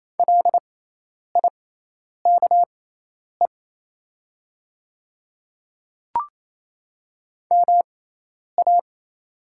Morse code, heard as a keyed tone, about 28 wpm, spaced as for 10 wpm, 700 Hz. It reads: LIKE MA
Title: 6x Character Spacing - Top 100 Words - 28wpm